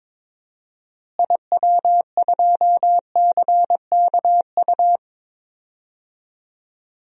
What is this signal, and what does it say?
Morse code, heard as a keyed tone, about 22 wpm, 700 Hz. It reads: IW2CKU
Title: DXCC-100 Call Signs - 6 Characters - Repeated 2x - 22wpm